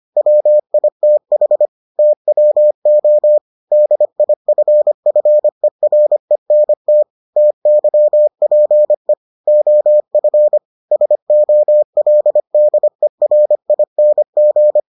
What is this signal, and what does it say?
Morse code, heard as a keyed tone, about 25 wpm, 600 Hz. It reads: WITH TWO DIFFERENT TYPE OF SOLDERING